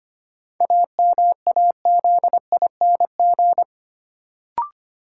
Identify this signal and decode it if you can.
Morse code, heard as a keyed tone, about 25 wpm, 700 Hz. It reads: AMAZING
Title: Sentences from Top 1500 Words - New Word Drill - 25wpm